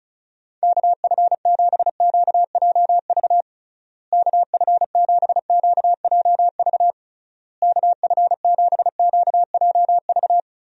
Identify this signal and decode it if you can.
Morse code, heard as a keyed tone, about 36 wpm, 700 Hz. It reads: KF7QJV KF7QJV KF7QJV
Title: USA Call Signs - 6 Characters - Repeated 3x - 35wpm